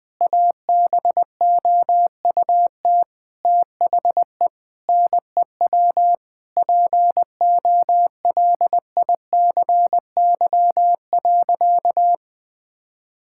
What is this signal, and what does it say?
Morse code, heard as a keyed tone, about 20 wpm, 700 Hz. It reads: ABOUT THE NEW POLICY.
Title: Sentences with 8-Word Phrases - 20wpm